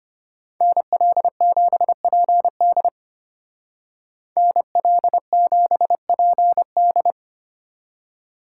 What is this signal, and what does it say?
Morse code, heard as a keyed tone, about 31 wpm, 700 Hz. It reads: NL7PD NL7PD